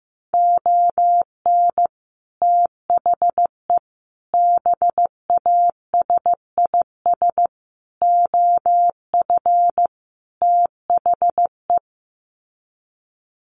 Morse code, heard as a keyed tone, about 15 wpm, 700 Hz. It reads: ON THE BASIS OF THE
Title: Sets of 5 Words 15wpm